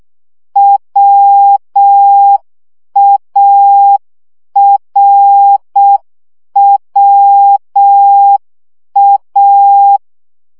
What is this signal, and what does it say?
Morse code, heard as a keyed tone, about 6 wpm, 800 Hz. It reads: WARWA
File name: CW gyakorlas a r w 5 baud